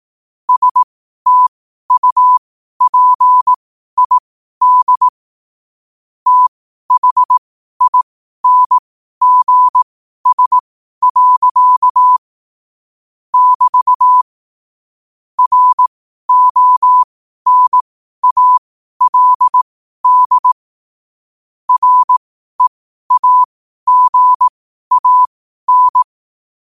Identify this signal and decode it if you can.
Morse code, heard as a keyed tone, about 18 wpm, spaced as for 12 wpm, 1000 Hz. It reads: STUPID THINGS. <BT> RONALD REAGAN
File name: qotdfiles / 12wpm